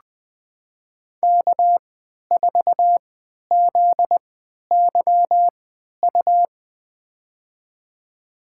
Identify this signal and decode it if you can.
Morse code, heard as a keyed tone, about 20 wpm, spaced as for 11 wpm, 700 Hz. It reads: K4ZYU
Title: USA Call Signs - 5 Characters - 3x Character Spacing - 20wpm